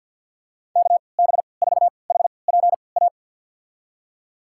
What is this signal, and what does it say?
Morse code, heard as a keyed tone, about 51 wpm, spaced as for 28 wpm, 700 Hz. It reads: KB4HPA
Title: USA Call Signs - 3x Character Spacing - 50wpm